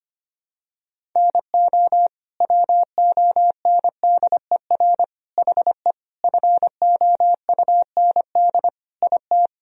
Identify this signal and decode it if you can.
Morse code, heard as a keyed tone, about 25 wpm, 700 Hz. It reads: NO WONDER HE FOUND IT
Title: Sentences from Top 1000 Words - Review Part 3 - 25wpm